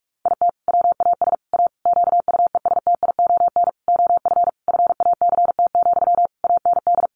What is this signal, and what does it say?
Morse code, heard as a keyed tone, about 44 wpm, 700 Hz. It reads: IT WAS A QUESTION OF FACT, AND
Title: Sentences from Top 1400 Words 45wpm